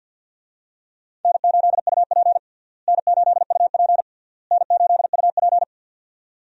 Morse code, heard as a keyed tone, about 47 wpm, 700 Hz. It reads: N8UP N8UP N8UP
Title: USA Call Signs - 4 Characters - Repeated 3x - 50wpm